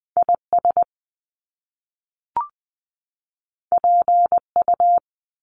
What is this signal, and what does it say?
Morse code, heard as a keyed tone, about 20 wpm, 700 Hz. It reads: IS PU
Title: Top Two Letter Combinations in Words 20wpm